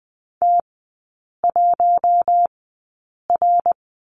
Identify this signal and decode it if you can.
Morse code, heard as a keyed tone, about 20 wpm, spaced as for 8 wpm, 700 Hz. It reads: T1R
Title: 060 Lesson 11e - Letter h - Three Characters 20wpm